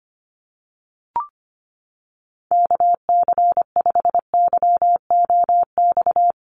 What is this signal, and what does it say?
Morse code, heard as a keyed tone, about 25 wpm, 700 Hz. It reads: KC5YOX
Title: USA Call Signs 25wpm